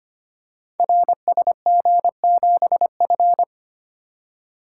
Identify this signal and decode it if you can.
Morse code, heard as a keyed tone, about 25 wpm, 700 Hz. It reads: RSG7F
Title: Code Groups - 5 Letter:Numbers - 25wpm